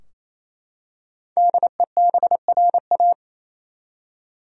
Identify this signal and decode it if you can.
Morse code, heard as a keyed tone, about 28 wpm, 700 Hz. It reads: DEBRA
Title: USA Common Names - 28wpm